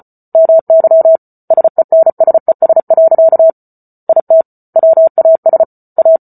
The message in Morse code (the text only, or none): MY SENSES. IT WAS A